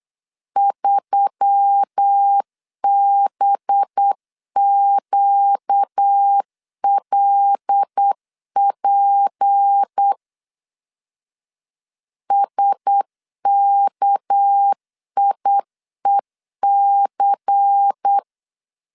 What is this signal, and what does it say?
Morse code, heard as a keyed tone, about 8 wpm, 800 Hz. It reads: 3BQLP SKIEC
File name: LEZIONI / 6 PKX0 / 6 PK 8wpm 3003